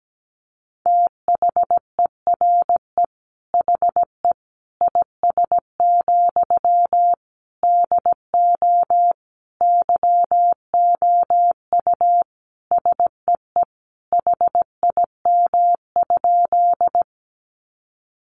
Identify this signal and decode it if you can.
Morse code, heard as a keyed tone, about 17 wpm, 700 Hz. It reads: THERE HE IS, DO YOU SEE HIM?
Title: Sentences from Top 100 Words 17wpm